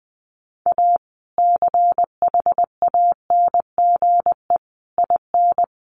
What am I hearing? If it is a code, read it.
Morse code, heard as a keyed tone, about 20 wpm, 700 Hz. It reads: A CHANGE IN
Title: Sentences from Top 1500 Words 20wpm